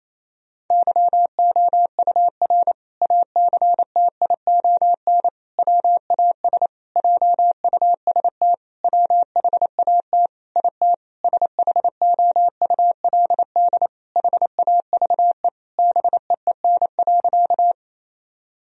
Morse code, heard as a keyed tone, about 27 wpm, 700 Hz. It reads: YOUR ACTION WAS JUST WHAT IT SHOULD HAVE BEEN.
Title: Sentences from Top 1000 Words - Review Part 4 - 28wpm